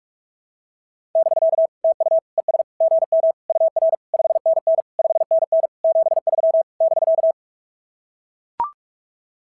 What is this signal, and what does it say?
Morse code, heard as a keyed tone, about 44 wpm, 690 Hz. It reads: <BK> TU ES GM UR 5NN 5NN 73 <BK>
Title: SOTA QSO Practice 45wpm